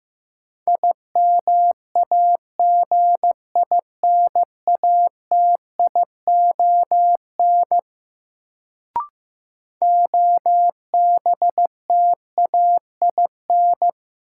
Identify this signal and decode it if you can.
Morse code, heard as a keyed tone, about 15 wpm, 700 Hz. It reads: IMAGINATION OBTAIN
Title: Sentences from Top 1300 Words - New Word Drill Alt - 15wpm